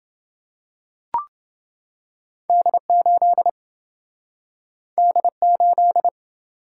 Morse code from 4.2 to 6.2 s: D8

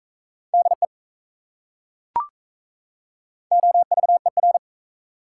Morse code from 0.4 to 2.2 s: DE

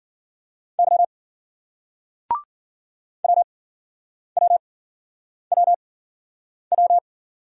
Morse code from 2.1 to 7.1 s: WWWW

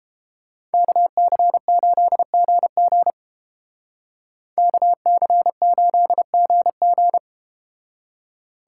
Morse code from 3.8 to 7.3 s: KC8GG

